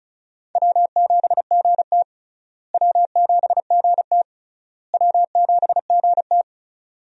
Morse code, heard as a keyed tone, about 36 wpm, 700 Hz. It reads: W7GT W7GT W7GT
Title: USA Call Signs - 4 Characters - Repeated 3x - 35wpm